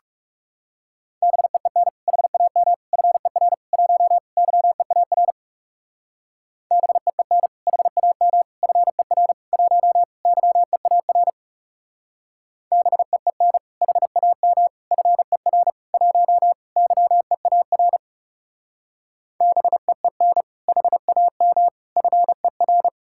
Morse code, heard as a keyed tone, about 34 wpm, 700 Hz. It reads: BEEN HAM FER 1 YEAR BEEN HAM FER 1 YEAR BEEN HAM FER 1 YEAR BEEN HAM FER
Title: Speed-Racing Common QSO Phrases - 30wpm